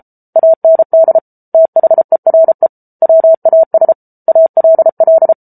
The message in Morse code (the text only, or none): AND THERE WAS ALL